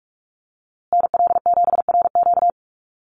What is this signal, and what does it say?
Morse code, heard as a keyed tone, about 44 wpm, 700 Hz. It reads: NL7RQ
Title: USA Call Signs Repeated 3x 45wpm